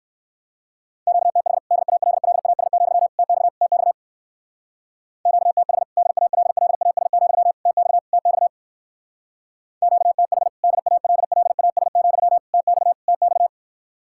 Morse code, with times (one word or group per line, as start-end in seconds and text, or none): 1.1-1.6 QTH
1.7-3.1 DALLAS,
3.2-3.5 TX
3.6-3.9 TX
5.3-5.8 QTH
6.0-7.5 DALLAS,
7.7-8.0 TX
8.1-8.5 TX
9.8-10.5 QTH
10.6-12.4 DALLAS,
12.5-12.9 TX
13.1-13.5 TX